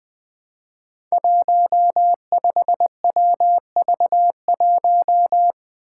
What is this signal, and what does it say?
Morse code, heard as a keyed tone, about 20 wpm, 700 Hz. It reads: 15WV1
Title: Five Challenging Letter-Number Combinations 20wpm